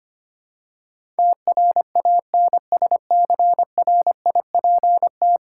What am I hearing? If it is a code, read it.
Morse code, heard as a keyed tone, about 25 wpm, 700 Hz. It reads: TRANSCRIPT
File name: Ten Letter Words - 25wpm